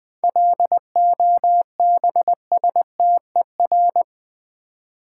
Morse code, heard as a keyed tone, about 20 wpm, 690 Hz. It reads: LOBSTER
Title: Easy Words with Hints - 20wpm